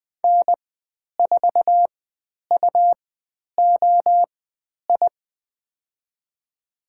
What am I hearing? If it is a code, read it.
Morse code, heard as a keyed tone, about 20 wpm, spaced as for 10 wpm, 700 Hz. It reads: N4UOI
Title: 102 Lesson 17g - Letter u - Call Signs 20wpm